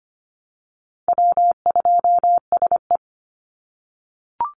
Morse code, heard as a keyed tone, about 25 wpm, 700 Hz. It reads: W2SE
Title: NA SOTA Chaser Callsigns 25wpm